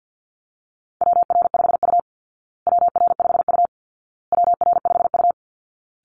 Morse code, heard as a keyed tone, about 48 wpm, 700 Hz. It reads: WR5U WR5U WR5U
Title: Call Signs Repeated 2x - 50wpm